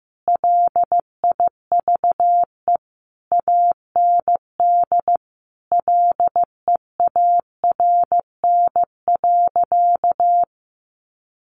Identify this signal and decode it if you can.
Morse code, heard as a keyed tone, about 15 wpm, 700 Hz. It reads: LIVE AND LEARN.
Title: Sentences from Top 800 Words 15wpm